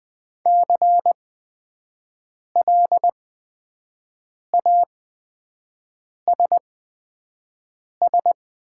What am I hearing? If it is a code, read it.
Morse code, heard as a keyed tone, about 20 wpm, spaced as for 5 wpm, 700 Hz. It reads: CLASS
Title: 8x Character Spacing - Top 500 Words - 20wpm